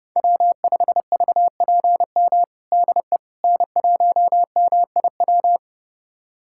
Morse code, heard as a keyed tone, about 30 wpm, 700 Hz. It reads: W5VPM DE N1MIW
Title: USA Callsign DE Callsign - 5 Characters - 30wpm